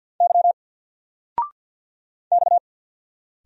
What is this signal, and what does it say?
Morse code, heard as a keyed tone, about 50 wpm, 700 Hz. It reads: Y X
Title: Single Letter-Number 50wpm